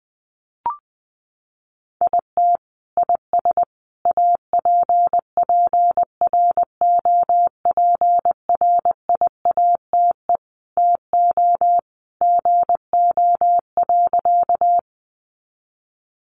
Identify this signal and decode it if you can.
Morse code, heard as a keyed tone, about 20 wpm, 700 Hz. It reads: IT IS APPROPRIATE TO GO.
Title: Sentences from Top 900 Words 20wpm